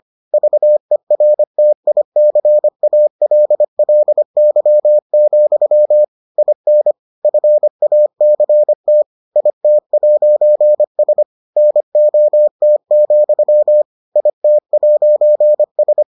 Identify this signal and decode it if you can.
Morse code, heard as a keyed tone, about 25 wpm, 600 Hz. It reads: VERTICALLY, IN FACT IT'S NOT, IT'S